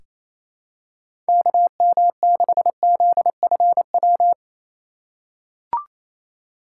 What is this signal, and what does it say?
Morse code, heard as a keyed tone, about 27 wpm, 700 Hz. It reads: KM6ZFW
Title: USA Call Signs - 6 Characters - 6x Character Spacing - 28wpm